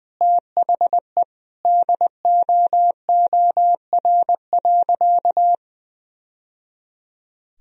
Morse code, heard as a keyed tone, about 21 wpm, 700 Hz. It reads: THE DOOR.
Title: Sentences from Top 1000 Words - Review Part 1 - 20wpm